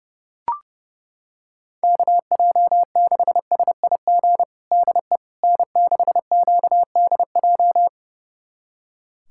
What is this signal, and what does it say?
Morse code, heard as a keyed tone, about 30 wpm, 700 Hz. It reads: KJ6SIG DE N6QDJ
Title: USA Callsign DE Callsign 30wpm